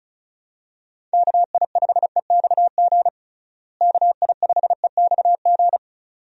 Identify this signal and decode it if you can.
Morse code, heard as a keyed tone, about 36 wpm, 700 Hz. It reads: KI5EXG KI5EXG
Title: USA Call Signs - 6 Characters - Repeated 2x - 35wpm